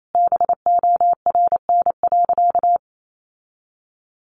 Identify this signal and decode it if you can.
Morse code, heard as a keyed tone, about 28 wpm, 700 Hz. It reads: BORN.
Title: Sentences from Top 1200 Words 28wpm